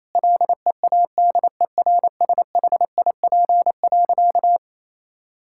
Morse code, heard as a keyed tone, about 27 wpm, 700 Hz. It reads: LEADERSHIP.
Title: Sentences from Top 1600 Words 28wpm